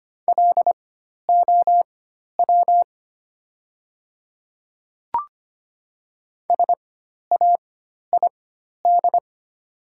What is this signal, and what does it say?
Morse code, heard as a keyed tone, about 25 wpm, spaced as for 12 wpm, 700 Hz. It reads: LOW SAID
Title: 4x Character Spacing - Top 200 Words - 25wpm